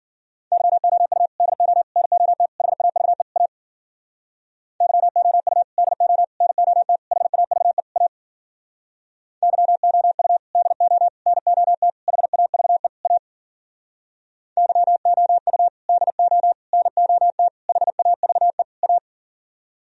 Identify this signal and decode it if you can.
Morse code, heard as a keyed tone, about 52 wpm, 700 Hz. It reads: YOU DO NOT HAVE A YOU DO NOT HAVE A YOU DO NOT HAVE A YOU DO NOT HAVE A